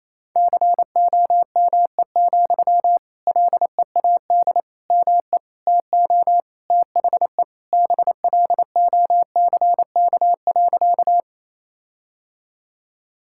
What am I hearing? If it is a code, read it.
Morse code, heard as a keyed tone, about 28 wpm, 700 Hz. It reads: COME, LEAD ME TO THE BLOCK.
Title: Sentences from Top 1600 Words 28wpm